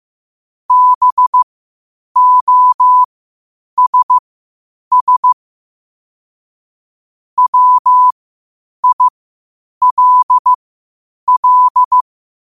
Morse code, heard as a keyed tone, about 15 wpm, spaced as for 8 wpm, 1000 Hz. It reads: BOSS WILL
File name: qotdfiles / 8wpm